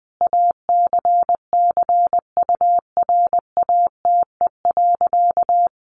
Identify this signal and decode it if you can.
Morse code, heard as a keyed tone, about 20 wpm, 700 Hz. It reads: ACCURATE.
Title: Sentences from Top 1200 Words 20wpm